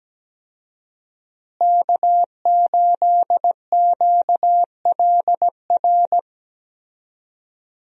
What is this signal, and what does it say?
Morse code, heard as a keyed tone, about 17 wpm, 700 Hz. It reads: K8QLR
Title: USA Call Signs - 5 Characters - Repeated 3x - 17wpm